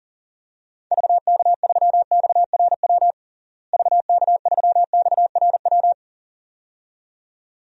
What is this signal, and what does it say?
Morse code, heard as a keyed tone, about 40 wpm, 700 Hz. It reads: VK3XRW VK3XRW
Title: DXCC-100 Call Signs - 6 Characters - Repeated 2x - 40wpm